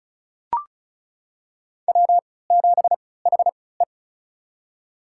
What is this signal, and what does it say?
Morse code, heard as a keyed tone, about 36 wpm, spaced as for 20 wpm, 700 Hz. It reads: W7HE
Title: USA Call Signs - 4 Characters - 3x Character Spacing - 35wpm